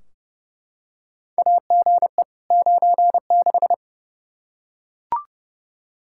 Morse code, heard as a keyed tone, about 32 wpm, 700 Hz. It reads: AGE 96
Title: Practice QSO - Exchange 3 - More Info - 30wpm